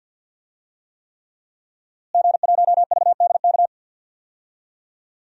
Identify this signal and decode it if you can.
Morse code, heard as a keyed tone, about 51 wpm, 700 Hz. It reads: G1VDK
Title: DXCC-100 Call Signs - 5 Characters - Repeated 3x - 50wpm